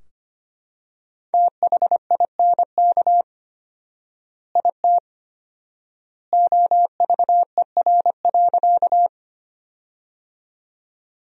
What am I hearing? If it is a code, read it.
Morse code, heard as a keyed tone, about 25 wpm, 700 Hz. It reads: THINK IT OVER.
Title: 4x Word-Spacing - Sentences from Top 200 Words - 25wpm